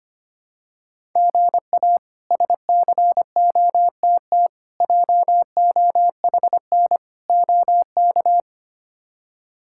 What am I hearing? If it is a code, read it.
Morse code, heard as a keyed tone, about 25 wpm, 700 Hz. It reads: GA SCOTT JOHN OK
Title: K1USN SST QSO Practice - 25wpm